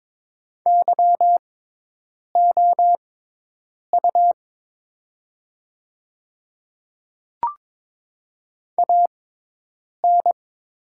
Morse code, heard as a keyed tone, about 23 wpm, spaced as for 8 wpm, 700 Hz. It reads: YOU AN
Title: 6x Character Spacing - Top 100 Words - 22wpm